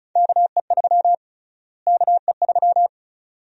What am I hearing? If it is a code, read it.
Morse code, heard as a keyed tone, about 36 wpm, 700 Hz. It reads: KE3 KE3
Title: USA Call Sign Prefixes - Repeated 2x - 35wpm